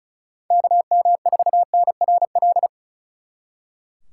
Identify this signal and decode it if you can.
Morse code, heard as a keyed tone, about 43 wpm, 700 Hz. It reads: KM4NRL
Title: Speed Racing - USA 2x3 Call Signs 35wpm